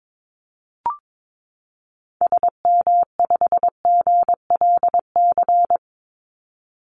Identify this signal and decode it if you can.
Morse code, heard as a keyed tone, about 22 wpm, 700 Hz. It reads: SM5GLC
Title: Call Signs - 22wpm